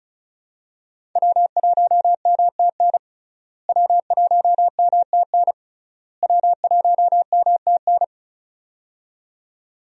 Morse code, heard as a keyed tone, about 36 wpm, 700 Hz. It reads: W1MTN W1MTN W1MTN
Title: USA Call Signs - 5 Characters - Repeated 3x - 35wpm